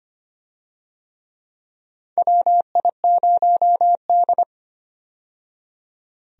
Morse code, heard as a keyed tone, about 25 wpm, 700 Hz. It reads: WI0D
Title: USA Call Signs - 4 Characters - 25wpm